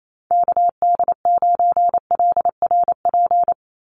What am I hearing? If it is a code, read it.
Morse code, heard as a keyed tone, about 28 wpm, 700 Hz. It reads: KD9LRP
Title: USA Call Signs 28wpm